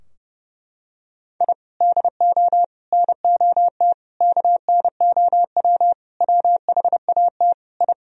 Morse code, heard as a keyed tone, about 32 wpm, 700 Hz. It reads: I DO NOT KNOW WHAT I
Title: Sentences from Top 1000 Words - Review Part 3 - 30wpm